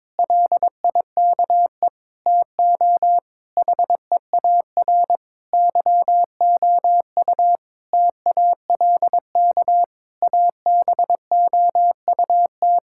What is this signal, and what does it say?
Morse code, heard as a keyed tone, about 22 wpm, 700 Hz. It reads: LIKE TO HEAR YOU TALK ABOUT
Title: Sentences from Top 1000 Words - Review Part 6 - 22wpm